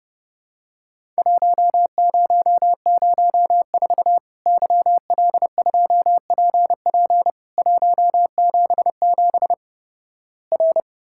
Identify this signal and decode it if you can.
Morse code, heard as a keyed tone, about 32 wpm, 700 Hz. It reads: 1004 YL2PP 177 R
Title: Worked All Europe DX Contest - QTC Exchange - 30wpm